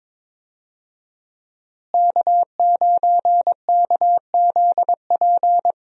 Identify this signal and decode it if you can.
Morse code, heard as a keyed tone, about 22 wpm, 700 Hz. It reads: K9KZP
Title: USA Call Signs - 5 Characters - Repeated 3x - 22wpm